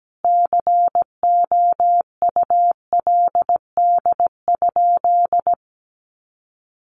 Morse code, heard as a keyed tone, about 17 wpm, 700 Hz. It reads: COULD?
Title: Sentences from Top 400 Words 17wpm